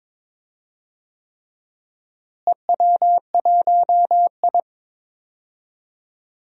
Morse code, heard as a keyed tone, about 22 wpm, 700 Hz. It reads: EW1I
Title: DXCC-100 Call Signs - 4 Characters - Repeated 3x - 22wpm